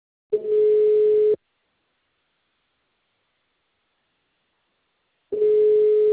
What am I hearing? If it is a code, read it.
Ringback tone (Europe, 425 Hz).